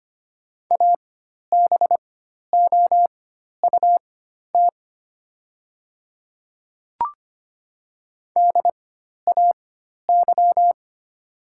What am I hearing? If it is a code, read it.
Morse code, heard as a keyed tone, about 25 wpm, spaced as for 12 wpm, 700 Hz. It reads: ABOUT DAY
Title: 4x Character Spacing - Top 200 Words - 25wpm